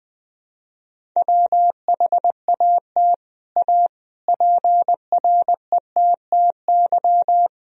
Morse code, heard as a keyed tone, about 20 wpm, 700 Hz. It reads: WHAT A PRETTY
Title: Sentences from Top 1000 Words - Review Part 1 - 20wpm